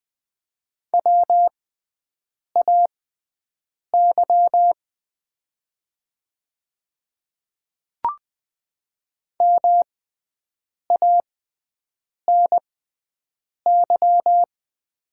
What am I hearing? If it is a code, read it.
Morse code, heard as a keyed tone, about 20 wpm, spaced as for 7 wpm, 700 Hz. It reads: WAY MANY
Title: 6x Character Spacing - Top 100 Words - 20wpm